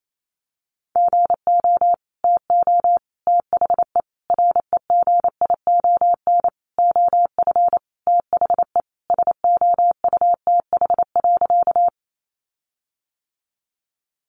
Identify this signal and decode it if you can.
Morse code, heard as a keyed tone, about 28 wpm, 700 Hz. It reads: GO TO THE REGION OF THE SOUTH.
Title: Sentences from Top 1500 Words 28wpm